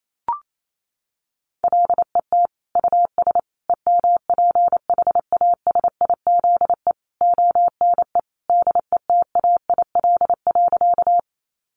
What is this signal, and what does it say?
Morse code, heard as a keyed tone, about 28 wpm, 700 Hz. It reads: LET US EMPHASIZE ONE DETAIL.
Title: Sentences from Top 1500 Words 28wpm